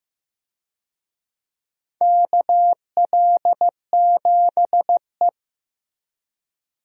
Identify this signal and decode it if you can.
Morse code, heard as a keyed tone, about 15 wpm, 700 Hz. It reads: KL7E
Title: DXCC-100 Call Signs - 4 Characters - 15wpm